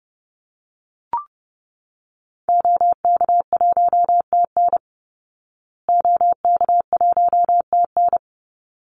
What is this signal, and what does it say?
Morse code, heard as a keyed tone, about 30 wpm, 700 Hz. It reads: OK1TN OK1TN
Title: Call Signs Repeated 3x - 30wpm